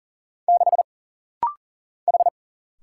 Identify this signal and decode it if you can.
Morse code, heard as a keyed tone, about 40 wpm, 700 Hz. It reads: 6 H
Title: Single Letter-Number Alt - Rapid-Fire 40wpm